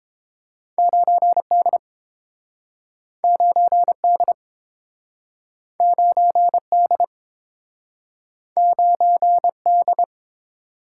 Morse code, heard as a keyed tone, about 32 wpm, 700 Hz. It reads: 9D 9D 9D 9D